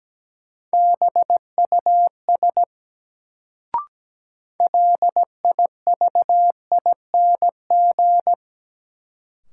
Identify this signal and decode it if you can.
Morse code, heard as a keyed tone, about 17 wpm, 700 Hz. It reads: BUS LIVING
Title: Sentences from Top 1000 Words - Review Part 6 - New Word Drill - 17wpm